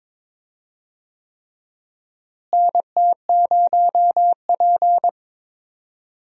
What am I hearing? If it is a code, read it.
Morse code, heard as a keyed tone, about 23 wpm, 700 Hz. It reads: NT0P
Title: USA Call Signs - 4 Characters - Repeated 3x - 22wpm